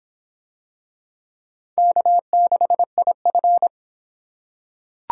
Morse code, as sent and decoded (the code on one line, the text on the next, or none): -.- -.... .. ..-.
K6IF